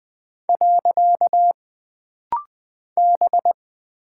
Morse code, heard as a keyed tone, about 20 wpm, 700 Hz. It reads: . B
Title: Single Character (Letter:Number:Punctuation:Prosign) - Rapid-Fire - 20wpm